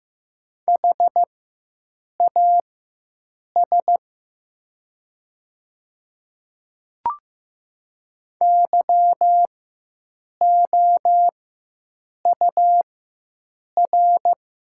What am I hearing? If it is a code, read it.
Morse code, heard as a keyed tone, about 15 wpm, spaced as for 7 wpm, 700 Hz. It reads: HAS YOUR
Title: 4x Character Spacing - Top 100 Words - 15wpm